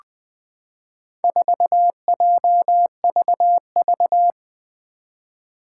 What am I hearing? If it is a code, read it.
Morse code, heard as a keyed tone, about 20 wpm, 700 Hz. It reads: 4JVV
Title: Four Challenging Letter-Number Combinations 20wpm